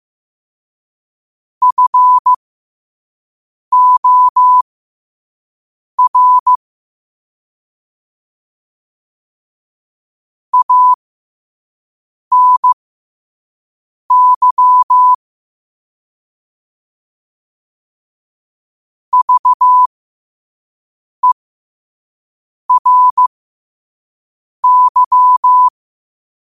Morse code, heard as a keyed tone, about 15 wpm, spaced as for 5 wpm, 1000 Hz. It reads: FOR ANY VERY
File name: qotdfiles / 5wpm